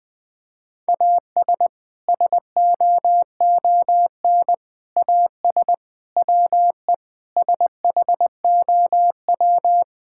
Morse code, heard as a keyed tone, about 19 wpm, 700 Hz. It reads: AS SOON AS WE SHOW
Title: Sentences from Top 1600 Words 20wpm